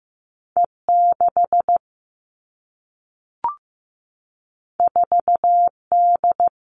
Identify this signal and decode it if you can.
Morse code, heard as a keyed tone, about 15 wpm, 700 Hz. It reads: E6 4D
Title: All Cut Numbers - 2-Digit 15wpm